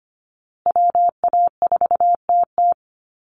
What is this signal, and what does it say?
Morse code, heard as a keyed tone, about 25 wpm, 700 Hz. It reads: WA4TT
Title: Call Signs Repeated 3x - 25wpm